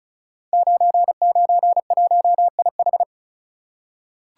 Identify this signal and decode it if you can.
Morse code, heard as a keyed tone, about 36 wpm, 700 Hz. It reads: 991IH
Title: Code Groups - 5 Letter:Numbers - 35wpm